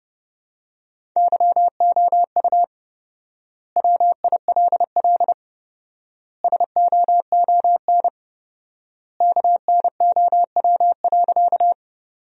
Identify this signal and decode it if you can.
Morse code, heard as a keyed tone, about 30 wpm, 700 Hz. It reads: YOU WILL SOON KNOW.
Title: 4x Word-Spacing - Sentences from Top 400 Words - 30wpm